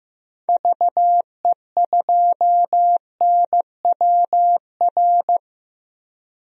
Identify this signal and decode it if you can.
Morse code, heard as a keyed tone, about 15 wpm, 700 Hz. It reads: VE2NWR
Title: DXCC-100 Call Signs - 6 Characters - 15wpm